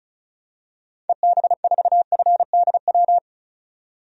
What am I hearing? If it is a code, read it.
Morse code, heard as a keyed tone, about 34 wpm, 700 Hz. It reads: EB4FDW
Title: DXCC-100 Call Signs - 6 Characters - 35wpm